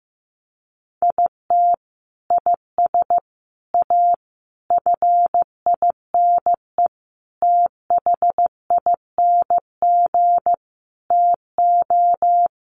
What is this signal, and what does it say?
Morse code, heard as a keyed tone, about 14 wpm, 700 Hz. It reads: IT IS A FINE THING TO
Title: Sentences from Top 1000 Words 15wpm